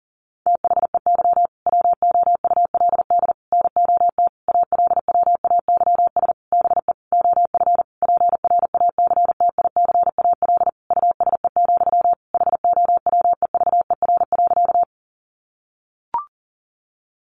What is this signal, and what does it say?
Morse code, heard as a keyed tone, about 40 wpm, 700 Hz. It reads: THEY WOULD NOT ALWAYS BE OF PRACTICAL USE, HOWEVER.
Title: Sentences from Top 1400 Words 40wpm